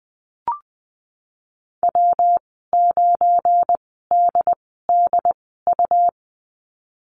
Morse code, heard as a keyed tone, about 20 wpm, spaced as for 15 wpm, 700 Hz. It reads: W9DDU
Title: 179 Lesson 29g - Slash - Call Signs 20wpm